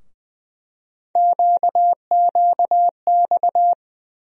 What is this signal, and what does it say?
Morse code, heard as a keyed tone, about 20 wpm, 700 Hz. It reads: QQX
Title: USA Call Sign Postfixes - 3 Letters 20wpm